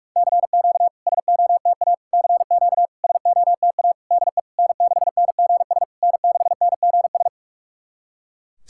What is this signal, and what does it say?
Morse code, heard as a keyed tone, about 48 wpm, 690 Hz. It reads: CQ SOTA CQ SOTA DE N6NGS N6NGS
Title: SOTA QSO Practice 45wpm